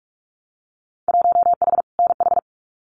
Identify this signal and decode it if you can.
Morse code, heard as a keyed tone, about 44 wpm, 700 Hz. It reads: 1H NH
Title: Winter Field Day Exchanges 45wpm